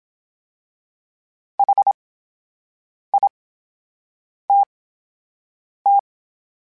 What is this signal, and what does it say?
Morse code, heard as a keyed tone, about 27 wpm, spaced as for 7 wpm, 800 Hz. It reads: HITT
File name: Special INGs / Special INGs-mp3s / VerySlowToFaster Ing @25